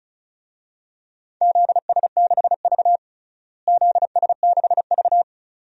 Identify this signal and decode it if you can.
Morse code, heard as a keyed tone, about 36 wpm, 700 Hz. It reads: ZS6V ZS6V
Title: DXCC-100 Call Signs - 4 Characters - Repeated 2x - 35wpm